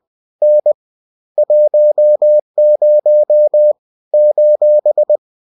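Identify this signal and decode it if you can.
Morse code, heard as a keyed tone, about 20 wpm, 600 Hz. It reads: N 10 :
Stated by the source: Bulletin sent in CW.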